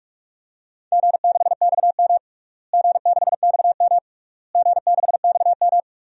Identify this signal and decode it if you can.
Morse code, heard as a keyed tone, about 47 wpm, 700 Hz. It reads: G6XM G6XM G6XM
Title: DXCC-100 Call Signs - 4 Characters - Repeated 3x - 45wpm